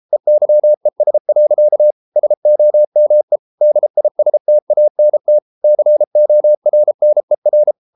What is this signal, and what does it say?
Morse code, heard as a keyed tone, about 33 wpm, 600 Hz. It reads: EYES. SOME DISTANT CORNER